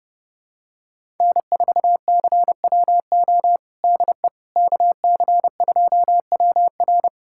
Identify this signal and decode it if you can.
Morse code, heard as a keyed tone, about 30 wpm, 700 Hz. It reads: N4CWO DE KC2WR
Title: USA Callsign DE Callsign - 5 Characters - 30wpm